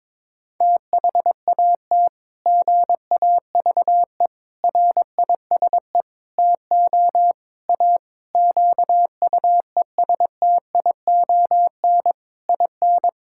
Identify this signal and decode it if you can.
Morse code, heard as a keyed tone, about 22 wpm, 700 Hz. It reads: THAT GAVE RISE TO A QUESTION IN
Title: Sentences from Top 1000 Words - Review Part 6 - 22wpm